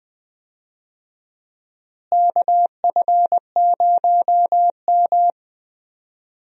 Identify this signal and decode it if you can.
Morse code, heard as a keyed tone, about 20 wpm, 700 Hz. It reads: KF0M
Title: USA Call Signs - 4 Characters - Repeated 2x - 20wpm